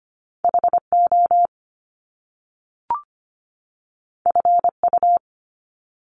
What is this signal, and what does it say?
Morse code, heard as a keyed tone, about 25 wpm, 700 Hz. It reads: HO FU
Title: Top Two Letter Combinations in Words 25wpm